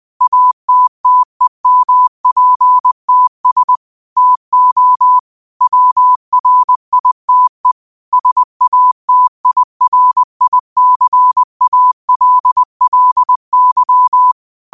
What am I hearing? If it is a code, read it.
Morse code, heard as a keyed tone, about 20 wpm, 1000 Hz. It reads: ATTEMPTS TO WRITE SATIRICALLY
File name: 20wpm